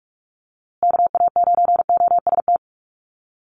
Morse code, heard as a keyed tone, about 44 wpm, 700 Hz. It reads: KA9OST
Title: USA Call Signs Repeated 2x 45wpm